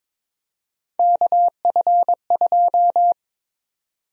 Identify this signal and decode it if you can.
Morse code, heard as a keyed tone, about 22 wpm, 700 Hz. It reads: KF2
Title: USA Call Sign Prefixes - 22wpm